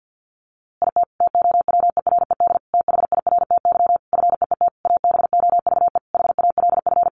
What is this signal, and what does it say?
Morse code, heard as a keyed tone, about 48 wpm, 700 Hz. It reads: IT TOWERED THIRTY FEET ABOVE HALF
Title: A Sound Of Thunder 50wpm